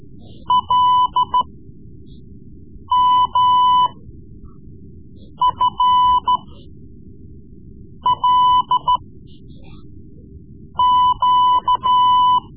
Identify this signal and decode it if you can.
Morse code, heard as a keyed tone, about 13 wpm, spaced as for 5 wpm, 1000 Hz. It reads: LMFLQ
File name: Morse Lessons / Morse Lessons / Tape-3B